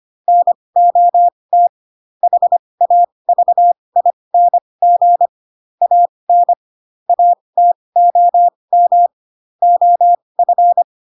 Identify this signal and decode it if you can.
Morse code, heard as a keyed tone, about 25 wpm, spaced as for 20 wpm, 700 Hz. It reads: NOT HAVING AN ATOM OF
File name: Pinocchio / Pinocchio ch2 15 25 ik1hge